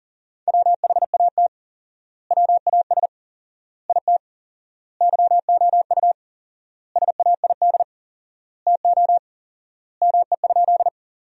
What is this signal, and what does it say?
Morse code, heard as a keyed tone, about 40 wpm, 700 Hz. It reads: WHAT WAS IT YOU SAID TO ME?
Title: 4x Word-Spacing - Sentences from Top 200 Words - 40wpm